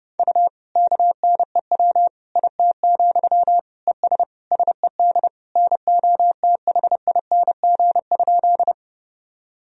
Morse code, heard as a keyed tone, about 32 wpm, 700 Hz. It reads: U KNEW IT, ES SED NOTHING?
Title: Sentences from Top 500 Words and Abbreviations 30wpm